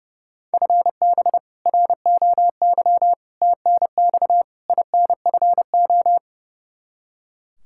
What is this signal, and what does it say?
Morse code, heard as a keyed tone, about 28 wpm, 700 Hz. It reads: FB ROY TNX INFO
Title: K1USN SST QSO Practice 30wpm